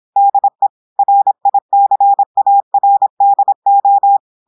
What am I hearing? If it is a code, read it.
Morse code, heard as a keyed tone, about 26 wpm, 800 Hz. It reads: DE RICARDO
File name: audios / portugais / portugais texte en clair 1 vitesse 26 mots minutes